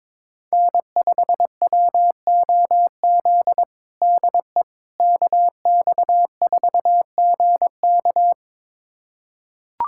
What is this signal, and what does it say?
Morse code, heard as a keyed tone, about 22 wpm, 700 Hz. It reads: N5WOZ DE KX4GK
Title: USA Callsign DE Callsign - 5 Characters - 22wpm